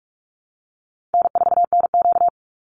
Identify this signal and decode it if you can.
Morse code, heard as a keyed tone, about 44 wpm, 700 Hz. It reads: N4NQ
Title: Easy Call Signs - 45wpm